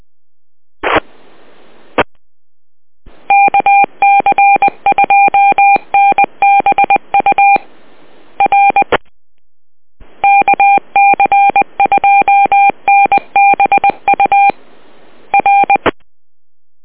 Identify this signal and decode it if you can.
Morse code, heard as a keyed tone, about 20 wpm, 800 Hz. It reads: E KC2NBU R KC2NBU R